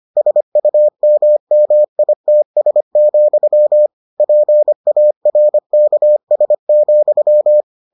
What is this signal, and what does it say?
Morse code, heard as a keyed tone, about 25 wpm, 600 Hz. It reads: SUMMITS, PARKS,